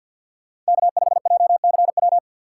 Morse code, heard as a keyed tone, about 60 wpm, 700 Hz. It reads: K5JCW
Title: Speed Racing - USA 1x3 Call Signs 50wpm